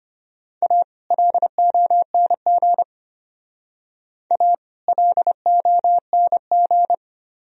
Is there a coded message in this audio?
Yes: Morse code from 0.6 to 6.9 s, heard as a keyed tone, about 32 wpm, 700 Hz. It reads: A LONG A LONG